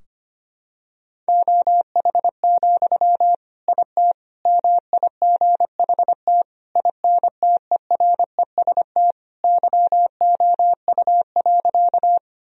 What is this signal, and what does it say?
Morse code, heard as a keyed tone, about 25 wpm, 700 Hz. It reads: OH, IT MIGHT INTEREST YOU.
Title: Sentences from Top 1000 Words - Review Part 2 - 25wpm